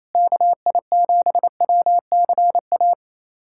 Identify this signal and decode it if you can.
Morse code, heard as a keyed tone, about 28 wpm, 700 Hz. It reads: KI7WCA
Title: USA Callsign DE Callsign 28wpm